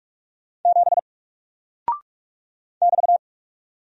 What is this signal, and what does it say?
Morse code, heard as a keyed tone, about 46 wpm, 700 Hz. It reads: 7 <BT>
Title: Single Character (Letter:Number:Punctuation:Prosign) 45wpm